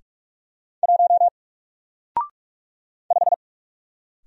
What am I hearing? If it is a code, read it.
Morse code, heard as a keyed tone, about 44 wpm, 700 Hz. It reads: 1 5
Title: Single Character (Letter:Number:Punctuation:Prosign) Alt 45wpm